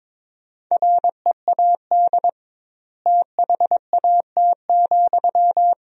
Morse code, heard as a keyed tone, about 22 wpm, 700 Hz. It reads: READ THAT,
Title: 2x Word-Spacing - Sentences from Top 200 Words - 22wpm